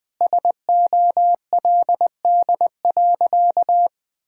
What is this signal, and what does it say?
Morse code, heard as a keyed tone, about 19 wpm, 700 Hz. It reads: SOLD.
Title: Sentences with 4-Word Phrases - Spoken Beginning - 20wpm